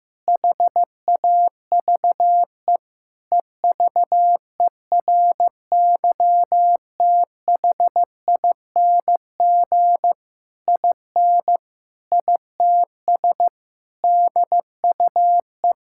Morse code, heard as a keyed tone, about 15 wpm, 700 Hz. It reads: HAVE EVERYTHING IN ITS DUE
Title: Sentences from Top 1000 Words - Review Part 6 - 15wpm